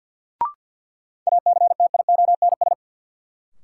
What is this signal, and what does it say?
Morse code, heard as a keyed tone, about 50 wpm, 700 Hz. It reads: ACTIONS
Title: Sentences from Top 1000 Words - Review Part 6 - New Word Drill - 50wpm